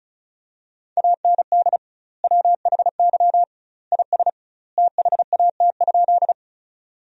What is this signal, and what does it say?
Morse code, heard as a keyed tone, about 36 wpm, 700 Hz. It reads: AND WHY IS THAT?
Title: 2x Word-Spacing - Sentences from Top 200 Words - 35wpm